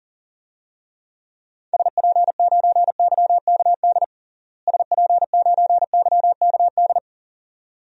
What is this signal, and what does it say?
Morse code, heard as a keyed tone, about 40 wpm, 700 Hz. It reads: SP9YKD SP9YKD
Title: DXCC-100 Call Signs - 6 Characters - Repeated 2x - 40wpm